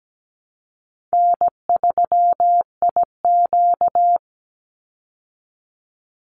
Morse code, heard as a keyed tone, about 17 wpm, 700 Hz. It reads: N3IQ
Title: Easy Call Signs Repeated 3x - 17wpm